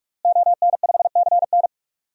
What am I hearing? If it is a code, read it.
Morse code, heard as a keyed tone, about 43 wpm, 700 Hz. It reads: ON5CN
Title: DXCC-100 Call Signs - 5 Characters - Repeated 3x - 45wpm